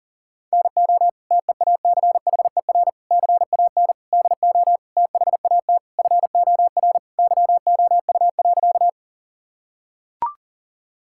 Morse code, heard as a keyed tone, about 40 wpm, 700 Hz. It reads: NO TEACHER CAN DO THAT FOR YOU.
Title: Sentences from Top 1000 Words - Review Part 6 - 40wpm